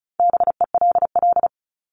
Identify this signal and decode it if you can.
Morse code, heard as a keyed tone, about 34 wpm, 700 Hz. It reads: BELL
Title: Top 1751-2000 Words 35wpm